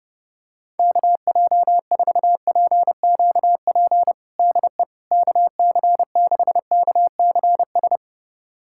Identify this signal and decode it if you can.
Morse code, heard as a keyed tone, about 30 wpm, 700 Hz. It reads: KJ4PQP DE KC6KCS
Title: USA Callsign DE Callsign - 6 Characters - 30wpm